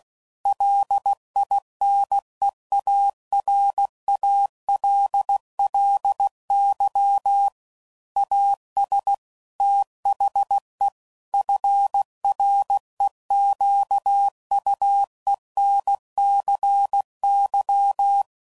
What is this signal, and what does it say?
Morse code, heard as a keyed tone, about 16 wpm, 800 Hz. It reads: LINEARALLY AS THE FREQUENCY
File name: CW Code Groups @15WPM / Technical Subjects / XL